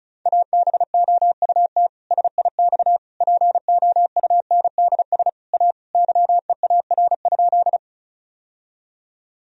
Morse code, heard as a keyed tone, about 36 wpm, 700 Hz. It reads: ABOUT SIX POUNDS A YEAR?
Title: Sentences from Top 1000 Words - Review Part 2 - 35wpm